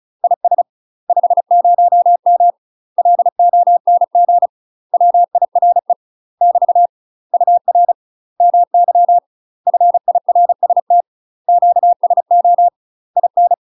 Morse code, heard as a keyed tone, about 34 wpm, 700 Hz. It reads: IS 50M LONG WIRE <BT> UR MY FIRST QSO IN
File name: UK MASTER QSOs TRAINING / 35WPM / UK MASTER QSOs 0040